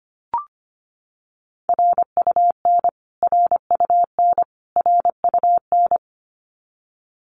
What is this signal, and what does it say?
Morse code, heard as a keyed tone, about 25 wpm, 700 Hz. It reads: RUN RUN RUN
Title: Top 500 Words Repeated 25wpm